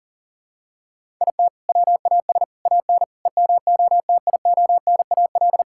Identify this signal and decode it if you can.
Morse code, heard as a keyed tone, about 40 wpm, 700 Hz. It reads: IT WAS AN EMOTIONAL
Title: Sentences from Top 1000 Words - Review Part 5 - 40wpm